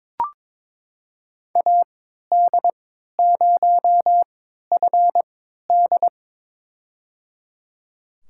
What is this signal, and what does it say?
Morse code, heard as a keyed tone, about 22 wpm, spaced as for 12 wpm, 700 Hz. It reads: AD0FD
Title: USA Call Signs - 5 Characters - 3x Character Spacing - 22wpm